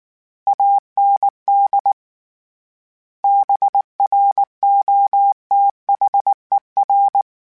Morse code, heard as a keyed tone, about 19 wpm, 800 Hz. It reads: AND BROTHER